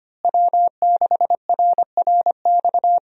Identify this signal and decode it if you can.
Morse code, heard as a keyed tone, about 25 wpm, 700 Hz. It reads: W6RRX
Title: USA Call Signs - 5 Characters - Repeated 2x - 25wpm